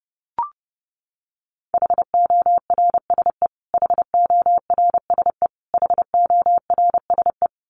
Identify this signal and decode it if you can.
Morse code, heard as a keyed tone, about 30 wpm, 700 Hz. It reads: HORSE HORSE HORSE
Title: Top 300 Words Repeated 30wpm